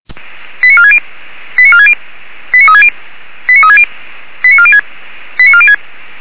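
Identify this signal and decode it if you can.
Five-tone selective call (ZVEI), heard as quick runs of five tones, tones of about 70 ms. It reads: 89460 89460 89360 89360 89406 89406
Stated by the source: ZVEI five-tone call 89288 received by SDR, logged by Feuerwehr Orlamünde.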